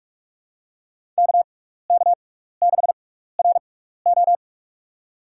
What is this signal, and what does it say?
Morse code, heard as a keyed tone, about 42 wpm, spaced as for 15 wpm, 700 Hz. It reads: KK6RO
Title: USA Call Signs - 5 Characters - 6x Character Spacing - 45wpm